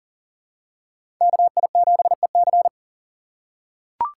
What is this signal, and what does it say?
Morse code, heard as a keyed tone, about 40 wpm, 700 Hz. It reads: KI7EC
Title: USA Call Signs - 5 Characters - 2x Character Spacing - 40wpm